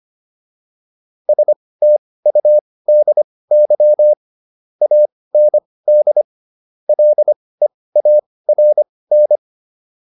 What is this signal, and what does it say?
Morse code, heard as a keyed tone, about 25 wpm, spaced as for 18 wpm, 600 Hz. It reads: STUDY AND LEARN